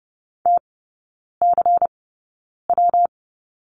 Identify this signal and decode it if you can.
Morse code, heard as a keyed tone, about 30 wpm, spaced as for 9 wpm, 700 Hz. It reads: TCW
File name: Lesson 19e - Letter w - Three Characters 30wpm